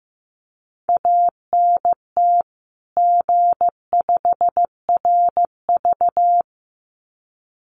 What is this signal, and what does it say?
Morse code, heard as a keyed tone, about 15 wpm, 700 Hz. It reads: ANT G5RV
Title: Common QSO Phrases 15wpm